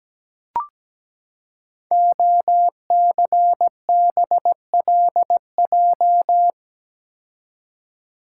Morse code, heard as a keyed tone, about 17 wpm, 700 Hz. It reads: OCBLJ
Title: Code Groups - 5 Letters - 17wpm